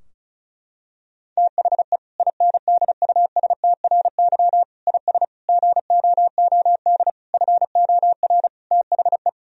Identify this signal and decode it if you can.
Morse code, heard as a keyed tone, about 36 wpm, 700 Hz. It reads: THE INDUSTRY IS GOOD FOR THE
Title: Sentences from Top 1000 Words - Review Part 4 - 35wpm